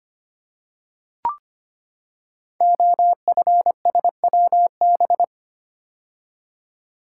Morse code, heard as a keyed tone, about 25 wpm, 700 Hz. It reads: OFSWB
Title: Code Groups - 5 Letters - 25wpm